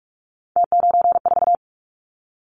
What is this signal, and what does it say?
Morse code, heard as a keyed tone, about 46 wpm, 700 Hz. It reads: T94
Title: Callsign Prefixes for Top 100 DXCC - 45wpm